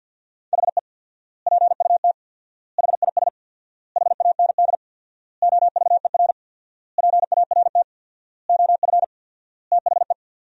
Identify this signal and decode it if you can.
Morse code, heard as a keyed tone, about 51 wpm, 700 Hz. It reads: HE PUT HIS HAND OVER PART OF THE
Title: 4x Word-Spacing - Sentences from Top 400 Words - 50wpm